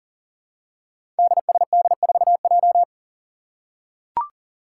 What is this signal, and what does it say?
Morse code, heard as a keyed tone, about 40 wpm, 700 Hz. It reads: DSD4J
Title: Five Challenging Letter-Number Combinations 40wpm